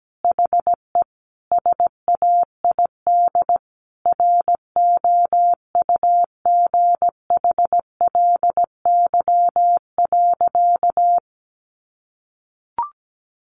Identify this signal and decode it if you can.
Morse code, heard as a keyed tone, about 17 wpm, 700 Hz. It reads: HE SAID ROUGHLY.
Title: Sentences from Top 1500 Words 17wpm